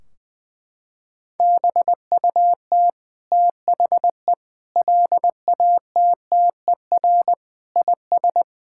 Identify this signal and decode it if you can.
Morse code, heard as a keyed tone, about 20 wpm, 700 Hz. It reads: BUT THE LATTER IS
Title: Sentences from Top 1600 Words 20wpm